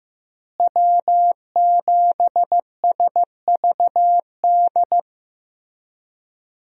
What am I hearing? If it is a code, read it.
Morse code, heard as a keyed tone, about 16 wpm, 700 Hz. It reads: W7SVD
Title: USA Call Signs - 5 Characters - Repeated 2x - 15wpm